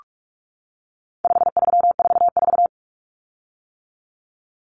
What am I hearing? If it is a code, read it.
Morse code, heard as a keyed tone, about 44 wpm, 700 Hz. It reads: 5344
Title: Common Cut Numbers - 4-Digit 45wpm